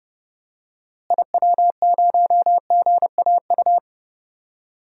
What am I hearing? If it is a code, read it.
Morse code, heard as a keyed tone, about 30 wpm, 700 Hz. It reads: IW0GAU
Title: DXCC-100 Call Signs - 6 Characters - Repeated 3x - 30wpm